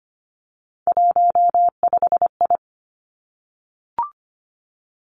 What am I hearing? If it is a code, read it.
Morse code, heard as a keyed tone, about 25 wpm, 700 Hz. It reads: 15I
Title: Three Challenging Letter-Number Combinations 25wpm